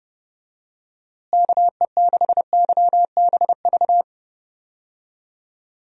Morse code, heard as a keyed tone, about 30 wpm, 700 Hz. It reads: KE6YBV
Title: USA Call Signs - 6 Characters - Repeated 3x - 30wpm